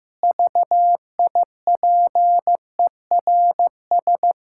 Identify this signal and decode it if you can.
Morse code, heard as a keyed tone, about 15 wpm, 700 Hz. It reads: VIPERS
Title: Pangram Challenge v2 - 15wpm